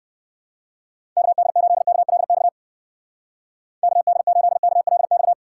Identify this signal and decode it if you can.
Morse code, heard as a keyed tone, about 68 wpm, 700 Hz. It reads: KD8KLX KD8KLX